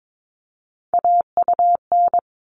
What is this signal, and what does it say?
Morse code, heard as a keyed tone, about 22 wpm, 700 Hz. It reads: AUN
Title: All Cut Numbers - 3-Digit 22wpm